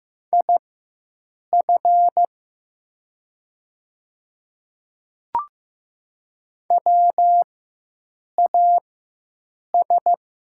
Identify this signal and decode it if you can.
Morse code, heard as a keyed tone, about 15 wpm, spaced as for 7 wpm, 700 Hz. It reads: IF WAS
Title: 4x Character Spacing - Top 100 Words - 15wpm